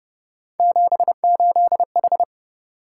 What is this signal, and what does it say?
Morse code, heard as a keyed tone, about 33 wpm, 700 Hz. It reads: 78H